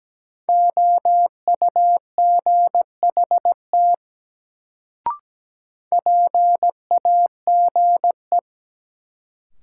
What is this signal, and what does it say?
Morse code, heard as a keyed tone, about 17 wpm, 700 Hz. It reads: OUGHT PAGE
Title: Sets of 5 Words Encore - New Word Drill - 17wpm